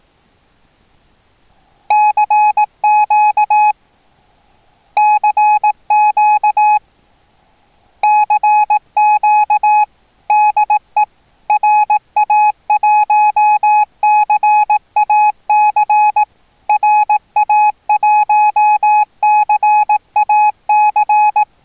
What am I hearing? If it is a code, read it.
Morse code, heard as a keyed tone, about 18 wpm, 840 Hz. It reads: CQ CQ CQ DE RA1CAC RA1CAC
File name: attachments / 1cq noagc 2cq agcin 3cq agcout